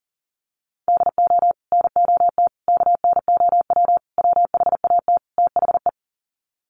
Morse code, heard as a keyed tone, about 40 wpm, 700 Hz. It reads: DO NOT KNOW WHAT THE
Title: Sets of 5 Words 40wpm